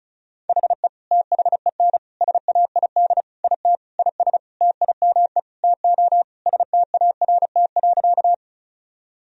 Morse code, heard as a keyed tone, about 36 wpm, 700 Hz. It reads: HE THEN SAID IT IS TIME TO START.
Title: Sentences from Top 1000 Words - Review Part 1 - 35wpm